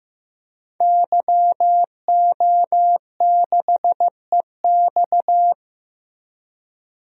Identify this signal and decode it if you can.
Morse code, heard as a keyed tone, about 16 wpm, 700 Hz. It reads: YO6EX
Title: DXCC-100 Call Signs - 5 Characters - Repeated 3x - 15wpm